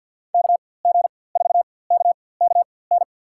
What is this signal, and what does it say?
Morse code, heard as a keyed tone, about 51 wpm, spaced as for 23 wpm, 700 Hz. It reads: KG4KKN
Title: USA Call Signs - 6 Characters - 4x Character Spacing - 50wpm